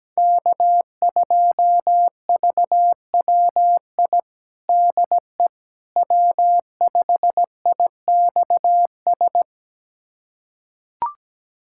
Morse code, heard as a keyed tone, about 17 wpm, 700 Hz. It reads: K2VWI DE W5IXS
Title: USA Callsign DE Callsign - 5 Characters - 17wpm